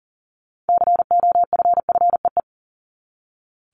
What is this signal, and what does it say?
Morse code, heard as a keyed tone, about 40 wpm, 700 Hz. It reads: COFFEE
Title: Top 1751-2000 Words 40wpm